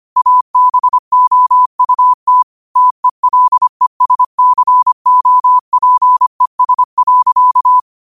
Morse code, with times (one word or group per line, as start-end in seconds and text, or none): 0.2-2.4 ABOUT
2.8-7.8 TELESCOPES.